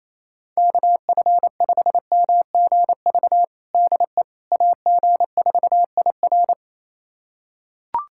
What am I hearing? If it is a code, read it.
Morse code, heard as a keyed tone, about 28 wpm, 700 Hz. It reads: KF5MGV DE AG4IR
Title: USA Callsign DE Callsign 28wpm